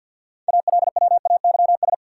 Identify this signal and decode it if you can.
Morse code, heard as a keyed tone, about 53 wpm, 700 Hz. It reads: ALWAYS
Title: Sentences from Top 300 Words - New Word Drill - 50wpm